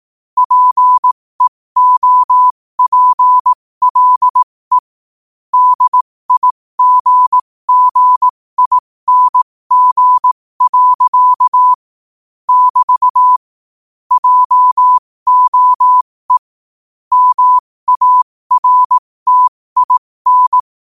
Morse code, heard as a keyed tone, about 18 wpm, spaced as for 15 wpm, 1000 Hz. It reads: PEOPLE DIGGING. <BT> JOE MARTIN
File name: qotdfiles / 15wpm